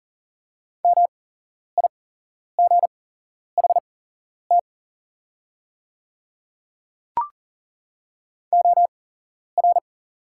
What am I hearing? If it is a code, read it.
Morse code, heard as a keyed tone, about 40 wpm, spaced as for 11 wpm, 700 Hz. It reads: MIGHT OR
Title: 8x Character Spacing - Top 300 Words - 40wpm